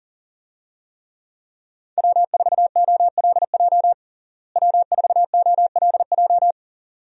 Morse code, heard as a keyed tone, about 40 wpm, 700 Hz. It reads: W4OLJ W4OLJ
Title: USA Call Signs - 5 Characters - Repeated 2x - 40wpm